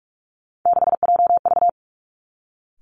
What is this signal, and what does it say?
Morse code, heard as a keyed tone, about 46 wpm, 700 Hz. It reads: 6JV
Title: Three Challenging Letter-Number Combinations 45wpm